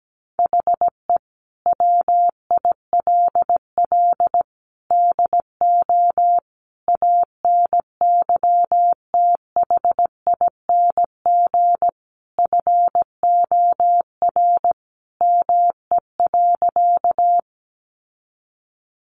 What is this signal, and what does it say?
Morse code, heard as a keyed tone, about 17 wpm, 700 Hz. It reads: HE WILL DO ANYTHING FOR ME.
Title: Sentences from Top 1000 Words 17wpm